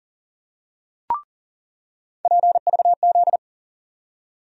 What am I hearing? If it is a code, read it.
Morse code, heard as a keyed tone, about 40 wpm, 700 Hz. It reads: PVZ
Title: USA Call Sign Postfixes - 3 Letters 40wpm